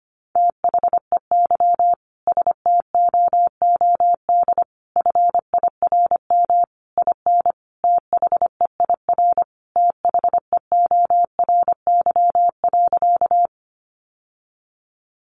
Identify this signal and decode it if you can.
Morse code, heard as a keyed tone, about 25 wpm, 700 Hz. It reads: THEY STOOD FIRM IN THEIR THEORY.
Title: Sentences from Top 1400 Words 25wpm